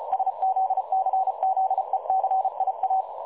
Morse code, heard as a keyed tone, about 33 wpm, 760 Hz. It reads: SZCZECIA
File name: dlazaawansowanych / nagraniazpasma / websdr recording 3565.0kHz